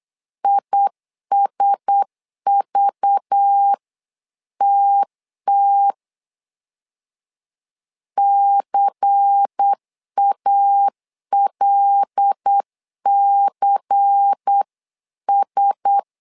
Morse code, heard as a keyed tone, about 8 wpm, 800 Hz. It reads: ISVTT CALCS
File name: LEZIONI / 5 LBVC / 5 LBVC 8wpm 2935